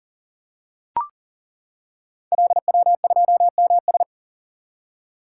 Morse code, heard as a keyed tone, about 42 wpm, 690 Hz. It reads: LW2MS
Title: Code Groups - 5 Letter:Numbers - 40wpm